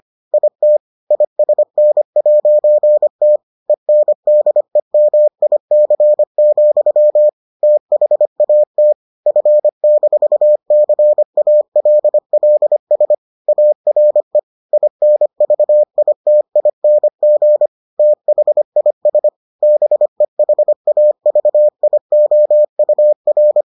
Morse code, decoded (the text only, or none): IT ISN'T ENDEMIC, THAT F-CALLS ARE INVITING THIS BEHAVIOUR